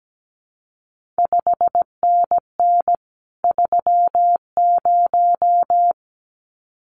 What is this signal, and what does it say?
Morse code, heard as a keyed tone, about 17 wpm, 700 Hz. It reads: 5NN 30
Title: CQ WW DX Contest Exchange Encore - 17wpm